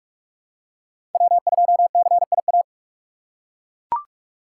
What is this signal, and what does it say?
Morse code, heard as a keyed tone, about 41 wpm, 700 Hz. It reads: W2CIA